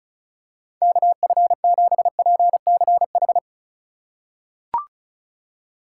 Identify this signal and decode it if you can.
Morse code, heard as a keyed tone, about 36 wpm, 700 Hz. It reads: KF7PCH
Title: USA Call Signs - 6 Characters - 2x Character Spacing - 35wpm